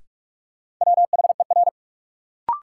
Morse code, heard as a keyed tone, about 46 wpm, 700 Hz. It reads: WHER
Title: Sentences from Top 1500 Words - New Word Drill Alt - 45wpm